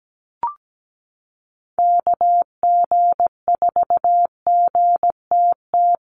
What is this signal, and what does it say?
Morse code, heard as a keyed tone, about 17 wpm, 700 Hz. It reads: KG4GTT
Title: USA Call Signs Repeated 2x 17wpm